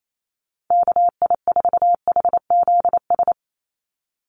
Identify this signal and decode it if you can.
Morse code, heard as a keyed tone, about 28 wpm, 700 Hz. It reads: KI4HZS
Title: USA Call Signs 28wpm